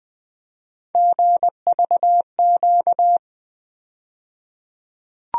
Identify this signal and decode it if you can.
Morse code, heard as a keyed tone, about 21 wpm, 700 Hz. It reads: GVQ
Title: USA Call Sign Postfixes - 3 Letters 20wpm